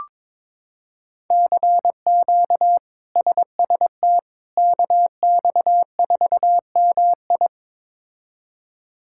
Morse code, heard as a keyed tone, about 22 wpm, 700 Hz. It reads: CQ SST KX4MI
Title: K1USN SST QSO Practice - 22wpm